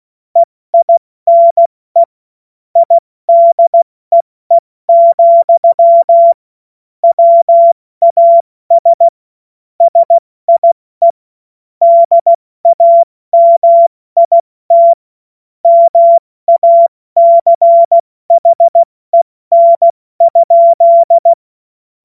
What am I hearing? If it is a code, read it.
Morse code, heard as a keyed tone, about 16 wpm, spaced as for 14 wpm, 680 Hz. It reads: EINE IDEE, WAS SIE DAMIT MACHEN?
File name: Morsekurs / Witze / Witz 28. Dezember 2021, 70 F80